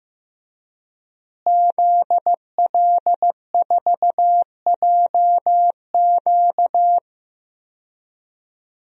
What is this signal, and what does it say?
Morse code, heard as a keyed tone, about 15 wpm, 700 Hz. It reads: ZL4JQ
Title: DXCC-100 Call Signs - 5 Characters - Repeated 3x - 15wpm